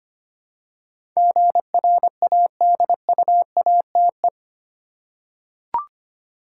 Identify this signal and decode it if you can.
Morse code, heard as a keyed tone, about 25 wpm, 700 Hz. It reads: GRADUATE
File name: 8 Letter Words - First 3 Letters Given - 25wpm